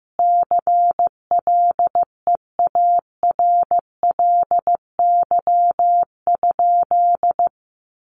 Morse code, heard as a keyed tone, about 15 wpm, 700 Hz. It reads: CLEARLY?
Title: Sentences from Top 1300 Words 15wpm